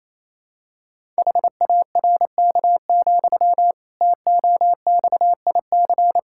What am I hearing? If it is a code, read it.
Morse code, heard as a keyed tone, about 28 wpm, 700 Hz. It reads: HARK, TOXIC
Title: Pangram Challenge v2 - 28wpm